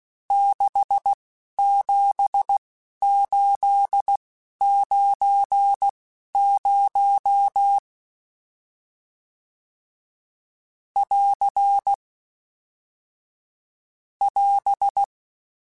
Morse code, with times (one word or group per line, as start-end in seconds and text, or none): 0.3-7.8 67890
11.0-11.9 <AR>
14.2-15.0 <AS>